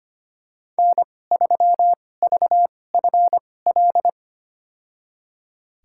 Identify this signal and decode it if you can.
Morse code, heard as a keyed tone, about 25 wpm, spaced as for 18 wpm, 700 Hz. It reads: N3VFL
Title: USA Call Signs - 2x Character Spacing - 25wpm